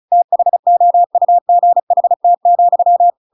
Morse code, heard as a keyed tone, about 35 wpm, 700 Hz. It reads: THOUGHT,
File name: Pinocchio / Pinocchio ch7 35 35 ik1hge